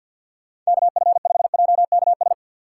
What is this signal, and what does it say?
Morse code, heard as a keyed tone, about 48 wpm, 700 Hz. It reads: KF5JKS
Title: USA Call Signs - 6 Characters - 3x Character Spacing - 50wpm